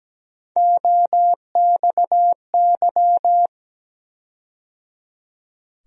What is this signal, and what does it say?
Morse code, heard as a keyed tone, about 17 wpm, 700 Hz. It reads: OXY
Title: USA Call Sign Postfixes - 3 Letters 17wpm